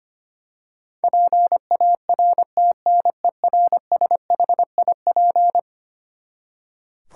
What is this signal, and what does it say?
Morse code, heard as a keyed tone, about 25 wpm, 700 Hz. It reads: PARTNERSHIP
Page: morse-code-ninja-practice-25wpm